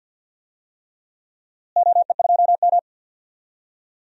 Morse code, heard as a keyed tone, about 53 wpm, 700 Hz. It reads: OE2M
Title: DXCC-100 Call Signs - 4 Characters - Repeated 3x - 50wpm